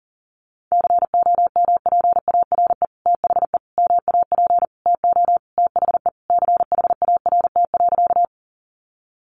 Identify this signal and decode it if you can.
Morse code, heard as a keyed tone, about 40 wpm, 700 Hz. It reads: COMPARE THE MAP TO THE CHART.
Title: Sentences from Top 1300 Words 40wpm